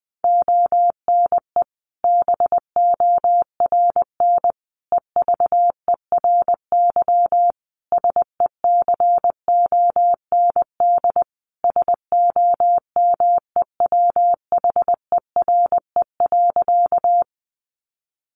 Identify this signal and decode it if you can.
Morse code, heard as a keyed tone, about 20 wpm, 700 Hz. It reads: ONE BORN EVERY SECOND SOMEWHERE.
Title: Sentences from Top 1400 Words 20wpm